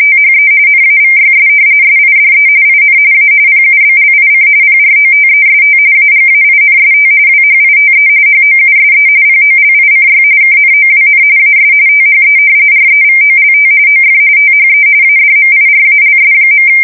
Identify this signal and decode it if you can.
RTTY, about 45.45 baud. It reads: THE NAVYS SIDE IN A PUBLIC PROTEST AGAINST WHITE HOUSE POLICY KNOWN AS THE REVOLT OF THE ADMIRALS.
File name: fsk170 / usnavy5